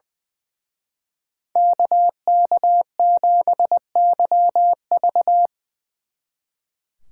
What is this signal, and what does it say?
Morse code, heard as a keyed tone, about 23 wpm, 700 Hz. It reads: KK7YV
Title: Speed Racing - USA 2x2 Call Signs 20wpm